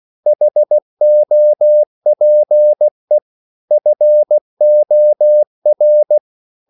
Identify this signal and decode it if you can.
Morse code, heard as a keyed tone, about 16 wpm, 600 Hz. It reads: HOPE FOR
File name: CW / Practice / EnglishSayings16